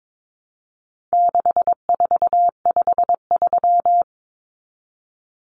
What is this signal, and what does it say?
Morse code, heard as a keyed tone, about 22 wpm, 700 Hz. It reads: 6453
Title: Numbers - 4-Digit 22wpm